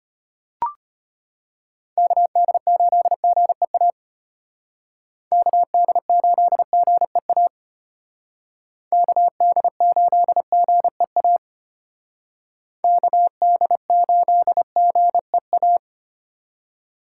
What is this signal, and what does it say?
Morse code, heard as a keyed tone, about 33 wpm, 700 Hz. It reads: KD8GEA KD8GEA KD8GEA KD8GEA